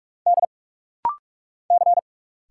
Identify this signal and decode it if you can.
Morse code, heard as a keyed tone, about 46 wpm, 700 Hz. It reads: D C
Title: Single Letter-Number - Rapid-Fire 45wpm